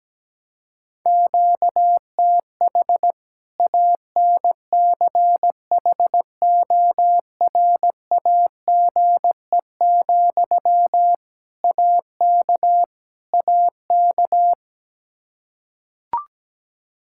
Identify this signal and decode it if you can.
Morse code, heard as a keyed tone, about 17 wpm, 700 Hz. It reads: QTH ANCHORAGE, AK AK
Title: Practice QSO - Exchange 1 - Beginning - 17wpm